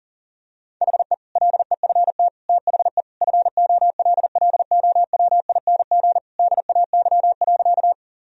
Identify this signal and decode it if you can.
Morse code, heard as a keyed tone, about 40 wpm, 700 Hz. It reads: HE LEFT THE FOLLOWING DAY.
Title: Sentences from Top 1000 Words - Review Part 4 - 40wpm